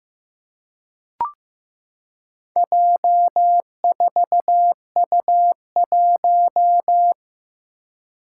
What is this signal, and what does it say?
Morse code, heard as a keyed tone, about 15 wpm, 700 Hz. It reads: J4U1
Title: Four Challenging Letter-Number Combinations 15wpm